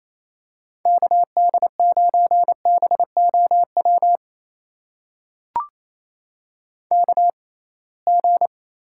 Morse code, heard as a keyed tone, about 28 wpm, spaced as for 21 wpm, 700 Hz. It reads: KD9BOW KG
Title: USA Call Signs - 6 Characters - 6x Character Spacing - 28wpm